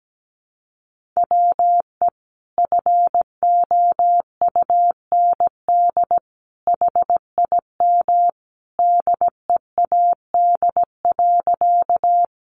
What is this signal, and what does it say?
Morse code, heard as a keyed tone, about 17 wpm, 700 Hz. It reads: WE FOUND HIM DEAD.
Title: Sentences from Top 1500 Words 17wpm